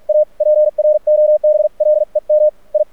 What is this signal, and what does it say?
Morse code, heard as a keyed tone, about 46 wpm, spaced as for 40 wpm, 610 Hz. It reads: A PROBLEM I
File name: archived / CRSnet